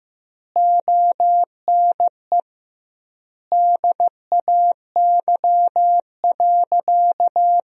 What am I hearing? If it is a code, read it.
Morse code, heard as a keyed tone, about 15 wpm, 700 Hz. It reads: ONE DAY.
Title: 2x Word-Spacing - Sentences from Top 300 Words - 15wpm